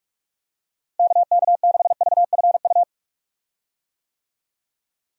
Morse code, heard as a keyed tone, about 46 wpm, 700 Hz. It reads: KK6VFU
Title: USA Call Signs - 6 Characters - Repeated 3x - 45wpm